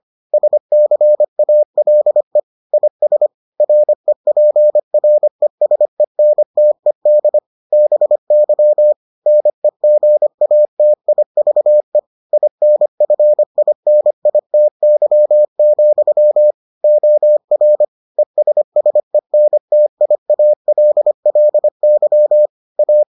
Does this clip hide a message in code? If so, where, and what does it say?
Yes: Morse code from 0.3 to 23.0 s, heard as a keyed tone, about 25 wpm, 600 Hz. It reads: SCALE IS REPRESENTED BY NEGATIVE INFINITY, OR ESSENTIALLY A